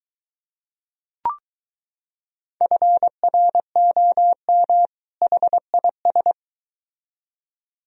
Morse code, heard as a keyed tone, about 23 wpm, 700 Hz. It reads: FROM HIS